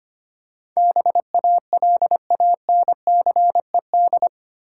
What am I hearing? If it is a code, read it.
Morse code, heard as a keyed tone, about 25 wpm, 700 Hz. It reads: BALANCED
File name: Eight Letter Words - 25wpm